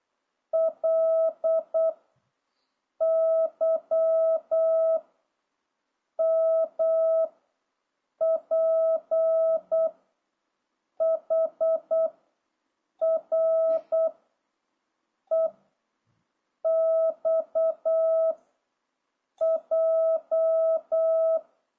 Morse code, heard as a keyed tone, about 8 wpm, 660 Hz. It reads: L Y M P H R E X J